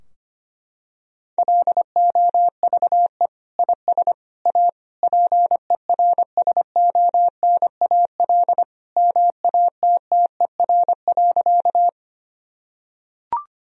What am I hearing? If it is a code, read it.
Morse code, heard as a keyed tone, about 25 wpm, 700 Hz. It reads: LOVE IS A PERSONAL MATTER.
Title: Sentences from Top 1000 Words - Review Part 4 - 25wpm